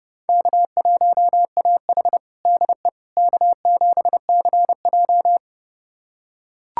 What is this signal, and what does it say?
Morse code, heard as a keyed tone, about 32 wpm, 700 Hz. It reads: K1AH DE K7CJ
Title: USA Callsign DE Callsign - 4 Characters - 30wpm